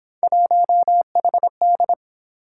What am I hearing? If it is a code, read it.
Morse code, heard as a keyed tone, about 27 wpm, 700 Hz. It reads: 1HD
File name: Speed-Racing - Three Letter-Number Combinations - 22wpm